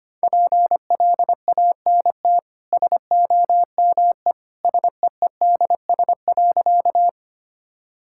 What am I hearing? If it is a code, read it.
Morse code, heard as a keyed tone, about 25 wpm, 700 Hz. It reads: PLANT SOME SEEDS.
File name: Easy Sentences with Hints - 25wpm